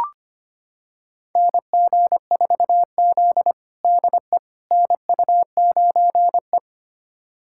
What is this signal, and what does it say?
Morse code, heard as a keyed tone, about 25 wpm, 700 Hz. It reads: NG4Z DE NU9E
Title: USA Callsign DE Callsign - 4 Characters - 25wpm